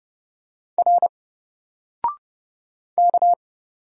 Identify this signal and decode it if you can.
Morse code, heard as a keyed tone, about 30 wpm, 700 Hz. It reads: R K
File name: Single-Letter-Number-30wpm